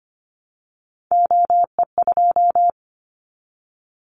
Morse code, heard as a keyed tone, about 25 wpm, 700 Hz. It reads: OE2
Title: Callsign Prefixes for Top 50 DXCC - 25wpm